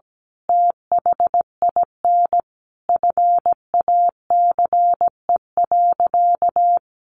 Morse code, heard as a keyed tone, about 17 wpm, 700 Hz. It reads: THIN FACE.
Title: A Sound Of Thunder 17wpm